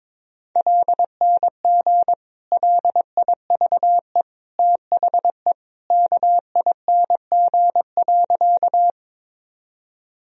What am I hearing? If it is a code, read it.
Morse code, heard as a keyed tone, about 22 wpm, 700 Hz. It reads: LNG LIVE THE KING.
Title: Sentences from Top 400 Words and Abbreviations 22wpm